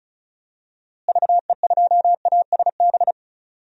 Morse code, heard as a keyed tone, about 34 wpm, 700 Hz. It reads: VE2ASB
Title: DXCC-100 Call Signs - 6 Characters - Repeated 3x - 35wpm